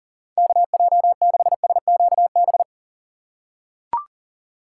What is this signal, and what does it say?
Morse code, heard as a keyed tone, about 40 wpm, 700 Hz. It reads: KJ6SQB
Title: USA Call Signs - 6 Characters - 4x Character Spacing - 40wpm